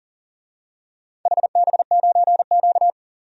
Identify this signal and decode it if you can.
Morse code, heard as a keyed tone, about 37 wpm, 700 Hz. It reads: HB9Q
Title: DXCC-100 Call Signs - 4 Characters - 40wpm